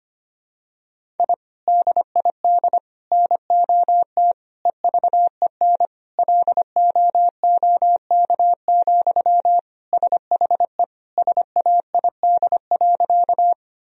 Morse code, heard as a keyed tone, about 25 wpm, 700 Hz. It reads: I DID NOT EVEN LOOK, SHE SAID.
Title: Sentences from Top 1000 Words - Review Part 1 - 25wpm